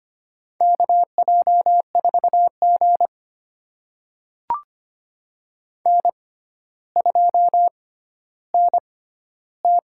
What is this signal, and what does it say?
Morse code, heard as a keyed tone, about 25 wpm, spaced as for 13 wpm, 700 Hz. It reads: KJ4G N2NT
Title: USA Call Signs - 4 Characters - 6x Character Spacing - 25wpm